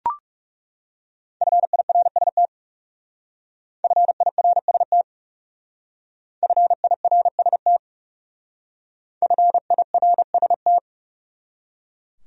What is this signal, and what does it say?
Morse code, heard as a keyed tone, about 36 wpm, 700 Hz. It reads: FIRST FIRST FIRST FIRST
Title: Speed-Racing Top 100 Words 30wpm